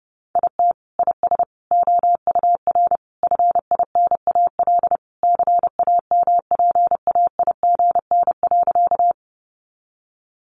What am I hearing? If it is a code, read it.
Morse code, heard as a keyed tone, about 30 wpm, 700 Hz. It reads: IT IS OUR FINAL CAMPAIGN.
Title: Sentences from Top 1200 Words 30wpm